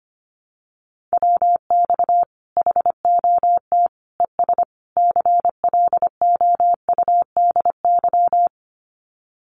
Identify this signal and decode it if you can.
Morse code, heard as a keyed tone, about 25 wpm, 700 Hz. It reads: WX HOT ES CLOUDY
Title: Common QSO Phrases 25wpm